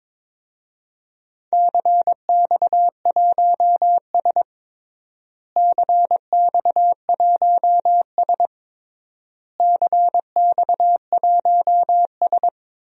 Morse code, heard as a keyed tone, about 22 wpm, 700 Hz. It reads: CX1S CX1S CX1S
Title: DXCC-100 Call Signs - 4 Characters - Repeated 3x - 22wpm